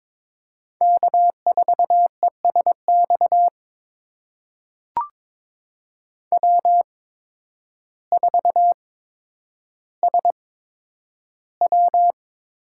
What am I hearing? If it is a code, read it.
Morse code, heard as a keyed tone, about 22 wpm, spaced as for 10 wpm, 700 Hz. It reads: K4ESX W4SW
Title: USA Call Signs - 5 Characters - 8x Character Spacing - 22wpm